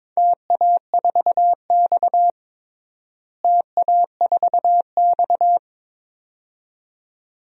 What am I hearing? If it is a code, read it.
Morse code, heard as a keyed tone, about 22 wpm, 700 Hz. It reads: TA4X TA4X
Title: DXCC-100 Call Signs - 4 Characters - Repeated 2x - 22wpm